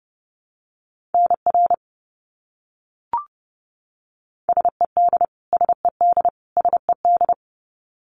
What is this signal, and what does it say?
Morse code, heard as a keyed tone, about 30 wpm, 700 Hz. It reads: NR SED SED SED
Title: Abbreviations Repeated 3x - 30wpm